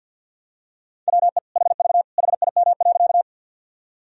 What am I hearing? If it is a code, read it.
Morse code, heard as a keyed tone, about 51 wpm, 700 Hz. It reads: WE HV HIM.
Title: Sentences from Top 100 Words and Abbreviations 50wpm